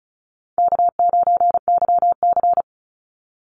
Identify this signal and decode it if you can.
Morse code, heard as a keyed tone, about 34 wpm, 700 Hz. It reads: K9YC
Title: Easy Call Signs Repeated 3x - 35wpm